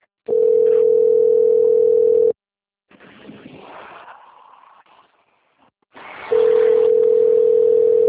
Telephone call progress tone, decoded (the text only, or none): ringback tone